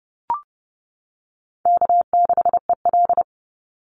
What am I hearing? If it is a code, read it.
Morse code, heard as a keyed tone, about 30 wpm, 700 Hz. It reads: K6EL
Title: NA SOTA Chaser Callsigns 30wpm